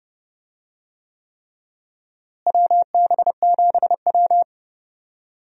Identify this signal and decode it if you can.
Morse code, heard as a keyed tone, about 30 wpm, 700 Hz. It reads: WB7W
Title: USA Call Signs - 4 Characters - Repeated 3x - 30wpm